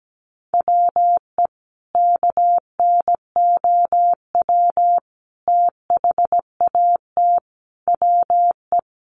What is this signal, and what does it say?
Morse code, heard as a keyed tone, about 17 wpm, 700 Hz. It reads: WE KNOW THAT WE
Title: Sentences from Top 1100 Words 17wpm